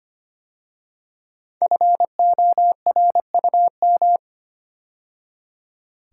Morse code, heard as a keyed tone, about 25 wpm, 700 Hz. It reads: FORUM
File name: 5 Letter Words - First 3 Letters Given - 25wpm